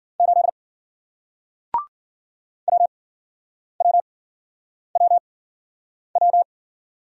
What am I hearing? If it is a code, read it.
Morse code, heard as a keyed tone, about 52 wpm, spaced as for 9 wpm, 700 Hz. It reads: 8 WWWW